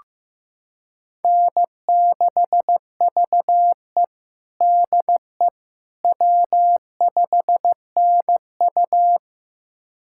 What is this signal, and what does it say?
Morse code, heard as a keyed tone, about 16 wpm, 700 Hz. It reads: N6VE DE W5NU
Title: USA Callsign DE Callsign - 4 Characters - 15wpm